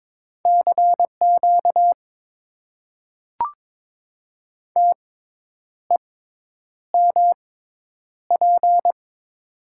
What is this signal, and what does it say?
Morse code, heard as a keyed tone, about 21 wpm, spaced as for 9 wpm, 700 Hz. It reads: CQ TEMP
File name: Top 90 QSO Elements - 6x Character Spacing - 22wpm